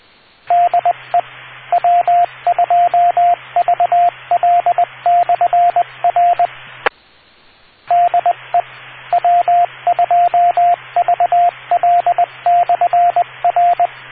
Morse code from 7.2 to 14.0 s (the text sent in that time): DE W2VL/R